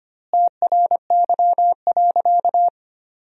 Morse code, heard as a keyed tone, about 25 wpm, 700 Hz. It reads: TRY.
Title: 2x Word-Spacing - Sentences from Top 200 Words - 25wpm